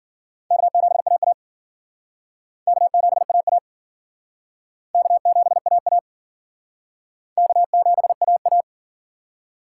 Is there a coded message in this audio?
Yes: Morse code from 0.5 to 8.6 s, heard as a keyed tone, about 51 wpm, 700 Hz. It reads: K7AA K7AA K7AA K7AA